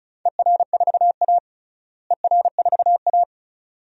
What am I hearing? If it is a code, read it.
Morse code, heard as a keyed tone, about 36 wpm, 700 Hz. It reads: ER4A ER4A
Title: DXCC-100 Call Signs - 4 Characters - Repeated 2x - 35wpm